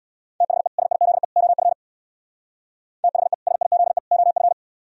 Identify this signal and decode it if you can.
Morse code, heard as a keyed tone, about 64 wpm, 700 Hz. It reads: THE SIZE OF THE SIZE OF